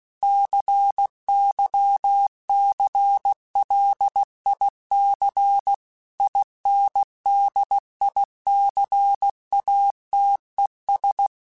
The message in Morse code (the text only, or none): CYCLIC INDICATES